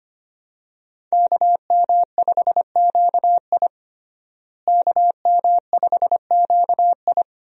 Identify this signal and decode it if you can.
Morse code, heard as a keyed tone, about 25 wpm, 700 Hz. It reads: KM5QI KM5QI
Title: USA Call Signs - 5 Characters - Repeated 3x - 25wpm